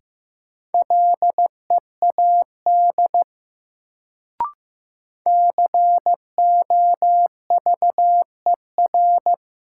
Morse code, heard as a keyed tone, about 15 wpm, 700 Hz. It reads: LEAD COVER
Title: Sentences from Top 500 Words - New Word Drill - 15wpm